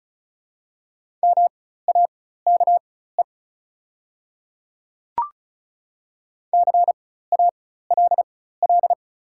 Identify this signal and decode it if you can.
Morse code, heard as a keyed tone, about 36 wpm, spaced as for 16 wpm, 700 Hz. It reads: MAKE CALL
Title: 4x Character Spacing - Top 100 Words - 35wpm